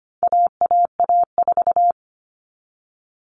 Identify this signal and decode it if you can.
Morse code, heard as a keyed tone, about 25 wpm, 700 Hz. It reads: AAA4
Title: All Cut Numbers - 4-Digit 25wpm